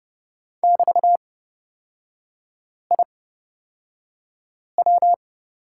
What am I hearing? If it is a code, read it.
Morse code, heard as a keyed tone, about 31 wpm, spaced as for 5 wpm, 700 Hz. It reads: <BT>IW
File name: Lesson 36e - BT - Three Characters 30wpm